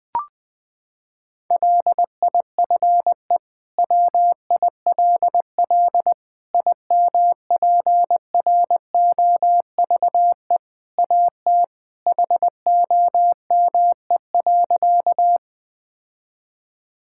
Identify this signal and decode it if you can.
Morse code, heard as a keyed tone, about 20 wpm, 700 Hz. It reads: LIFE WILL IMPROVE AT HOME.
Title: Sentences from Top 1000 Words - Review Part 5 - 20wpm